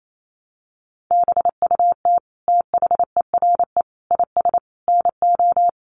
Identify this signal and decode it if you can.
Morse code, heard as a keyed tone, about 28 wpm, 700 Hz. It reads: BUT THERE IS NO
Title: Sentences from Top 1300 Words 28wpm